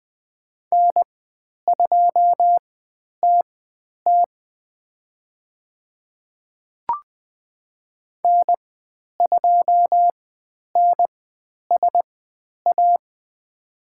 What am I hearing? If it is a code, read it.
Morse code, heard as a keyed tone, about 20 wpm, spaced as for 10 wpm, 700 Hz. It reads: N2TT N2NSA
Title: 095 Lesson 16g - Letter c - Call Signs 20wpm